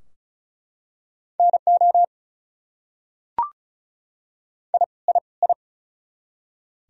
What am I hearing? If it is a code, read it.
Morse code, heard as a keyed tone, about 36 wpm, 700 Hz. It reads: NO I I I
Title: Top 100 Words Repeated 35wpm